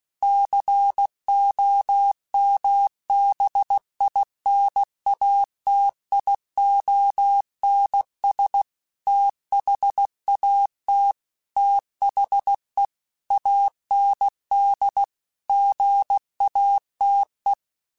Morse code, heard as a keyed tone, about 16 wpm, 800 Hz. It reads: COMBINATIONS THAT THE AND GATE
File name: Technical Subjects / Technical Subjects-mp3s / AND Gate @15